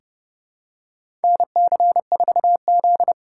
Morse code, heard as a keyed tone, about 30 wpm, 700 Hz. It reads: NC4Z
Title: USA Call Signs - 4 Characters - 2x Character Spacing - 30wpm